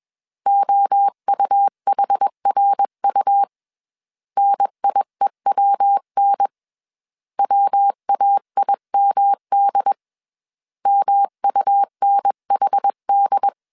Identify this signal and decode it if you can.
Morse code, heard as a keyed tone, about 21 wpm, 800 Hz. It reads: OUHRU NIEWN WAIMD MUNHD
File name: CW / DET / 1 3 1500C S B 20wpm 1754